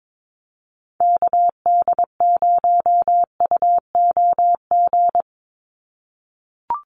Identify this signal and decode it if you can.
Morse code, heard as a keyed tone, about 22 wpm, 700 Hz. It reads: KD0UOG
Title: USA Call Signs 22wpm